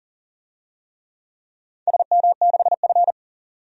Morse code, heard as a keyed tone, about 37 wpm, 700 Hz. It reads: SM6F
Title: DXCC-100 Call Signs - 4 Characters - 40wpm